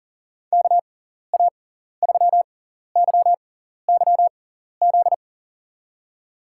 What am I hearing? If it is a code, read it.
Morse code, heard as a keyed tone, about 40 wpm, spaced as for 14 wpm, 700 Hz. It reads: KA3YYZ
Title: USA Call Signs - 6 Characters - 6x Character Spacing - 40wpm